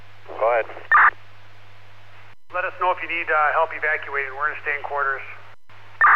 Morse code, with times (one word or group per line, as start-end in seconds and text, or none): none